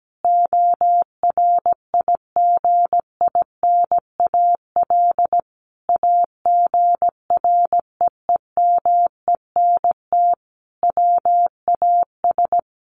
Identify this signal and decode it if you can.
Morse code, heard as a keyed tone, about 17 wpm, 700 Hz. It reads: ORIGINAL AGREEMENT WAS
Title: Sentences from Top 1500 Words 17wpm